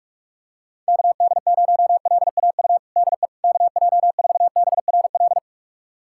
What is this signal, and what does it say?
Morse code, heard as a keyed tone, about 41 wpm, 690 Hz. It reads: KD0LAU DE KJ4BRL
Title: USA Callsign DE Callsign 45wpm